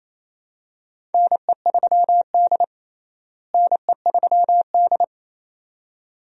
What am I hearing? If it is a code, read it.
Morse code, heard as a keyed tone, about 28 wpm, 700 Hz. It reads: NE3D NE3D
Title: USA Call Signs - 4 Characters - Repeated 2x - 28wpm